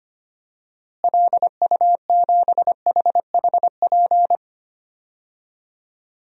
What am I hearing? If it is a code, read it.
Morse code, heard as a keyed tone, about 25 wpm, 700 Hz. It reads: LU7HHP
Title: DXCC-100 Call Signs - 6 Characters - Repeated 2x - 25wpm